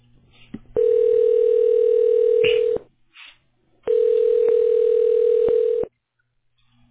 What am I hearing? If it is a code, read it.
Ringback tone (North America), 440 + 480 Hz.